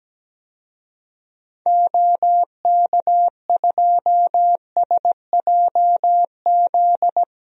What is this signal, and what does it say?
Morse code, heard as a keyed tone, about 17 wpm, 700 Hz. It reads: OK2SJZ
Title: DXCC-100 Call Signs - 6 Characters - 17wpm